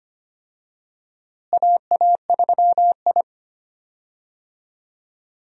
Morse code, heard as a keyed tone, about 25 wpm, 700 Hz. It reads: AA3I
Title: USA Call Signs - 4 Characters - Repeated 2x - 25wpm